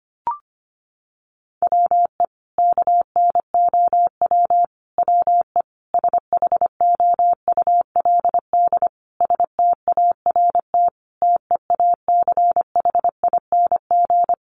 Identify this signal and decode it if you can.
Morse code, heard as a keyed tone, about 25 wpm, 700 Hz. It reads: WE KNOW WE SHOULD START TEACHING
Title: Sentences from Top 1100 Words 25wpm